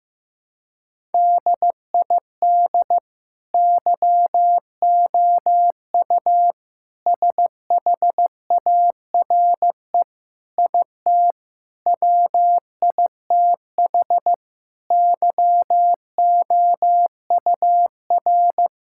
Morse code, heard as a keyed tone, about 15 wpm, 700 Hz. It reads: DID YOU SHARE IT WITH YOUR
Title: Sentences from Top 1000 Words - Review Part 6 - 15wpm